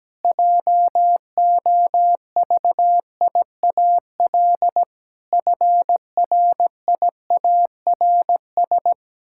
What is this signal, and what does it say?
Morse code, heard as a keyed tone, about 17 wpm, 700 Hz. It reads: JOVIAL FRIARS
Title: Pangram Challenge v2 - 17wpm